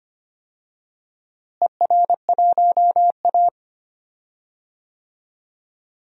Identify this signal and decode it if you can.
Morse code, heard as a keyed tone, about 25 wpm, 700 Hz. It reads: ER1A
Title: DXCC-100 Call Signs - 4 Characters - Repeated 3x - 25wpm